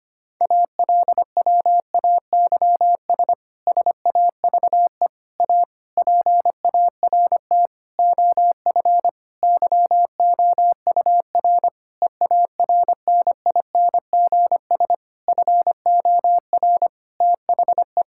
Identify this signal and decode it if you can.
Morse code, heard as a keyed tone, about 25 wpm, 700 Hz. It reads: ALWAYS SAVE A PART OF YOUR EARNINGS FOR THE
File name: practice / 25wpm / Difficult Sentences with Hints - 25wpm